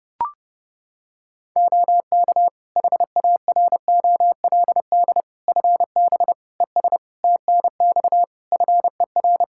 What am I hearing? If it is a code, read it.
Morse code, heard as a keyed tone, about 31 wpm, 700 Hz. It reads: OK HAROLD FB ES TNX FER
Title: Practice QSO - Exchange 2 - Info - 30wpm